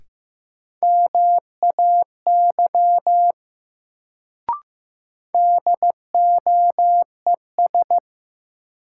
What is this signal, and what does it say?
Morse code, heard as a keyed tone, about 15 wpm, 700 Hz. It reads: MAY DOES
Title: Sets of 2 Words - New Word Drill - 15wpm